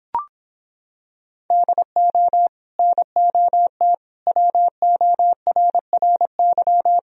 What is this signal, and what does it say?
Morse code, heard as a keyed tone, about 26 wpm, 700 Hz. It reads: DO NOT WORRY